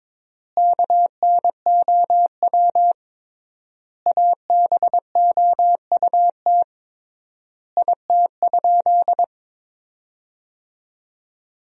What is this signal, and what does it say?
Morse code, heard as a keyed tone, about 22 wpm, 700 Hz. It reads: KNOW ABOUT IT?
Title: 3x Word-Spacing - Sentences from Top 100 Words - 22wpm